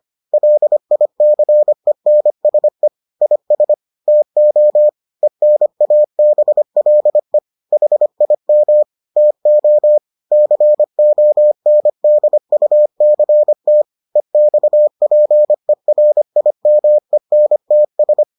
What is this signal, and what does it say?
Morse code, heard as a keyed tone, about 25 wpm, 600 Hz. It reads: LICENSE IS TO ENABLE HIM TO CONDUCT EXPERIMENTS